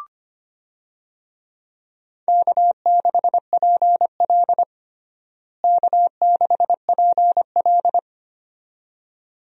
Morse code, heard as a keyed tone, about 25 wpm, 700 Hz. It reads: K6PL K6PL
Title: USA Call Signs - 4 Characters - Repeated 2x - 25wpm